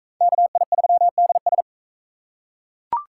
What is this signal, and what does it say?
Morse code, heard as a keyed tone, about 40 wpm, 700 Hz. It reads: KI3DS